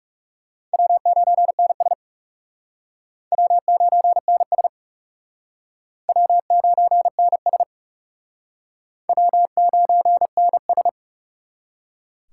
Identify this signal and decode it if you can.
Morse code, heard as a keyed tone, about 40 wpm, 700 Hz. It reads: W9NS W9NS W9NS W9NS